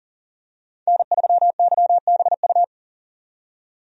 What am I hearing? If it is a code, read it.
Morse code, heard as a keyed tone, about 40 wpm, 700 Hz. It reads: N3YBU
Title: USA Call Signs - 5 Characters - 40wpm